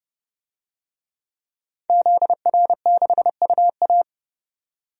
Morse code, heard as a keyed tone, about 32 wpm, 690 Hz. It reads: ZR6UA
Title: DXCC-100 Call Signs - 5 Characters - 30wpm